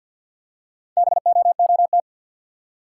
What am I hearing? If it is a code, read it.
Morse code, heard as a keyed tone, about 50 wpm, 700 Hz. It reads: BOOT
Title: Four Letter Words - 50wpm